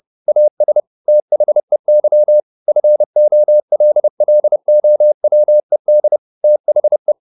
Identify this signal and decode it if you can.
Morse code, heard as a keyed tone, about 30 wpm, 600 Hz. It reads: AS THEY FOLLOWED THE
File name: the old man and the sea / the old man and the sea 1 30wpm